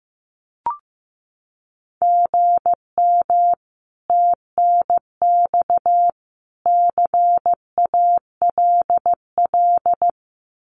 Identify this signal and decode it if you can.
Morse code, heard as a keyed tone, about 15 wpm, 700 Hz. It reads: GM TNX CALL
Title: Common QSO Phrases 15wpm